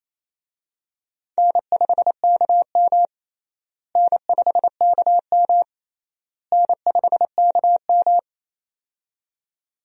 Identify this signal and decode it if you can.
Morse code, heard as a keyed tone, about 28 wpm, 700 Hz. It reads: N5KM N5KM N5KM
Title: USA Call Signs - 4 Characters - Repeated 3x - 28wpm